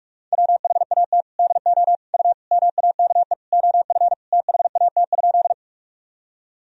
Morse code, heard as a keyed tone, about 47 wpm, 700 Hz. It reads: WHAT DO U MAKE OF THAT?
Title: Sentences from Top 100 Words and Abbreviations 45wpm